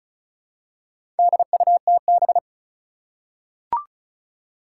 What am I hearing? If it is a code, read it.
Morse code, heard as a keyed tone, about 34 wpm, 700 Hz. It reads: DUTB
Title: Four Challenging Letter-Number Combinations 35wpm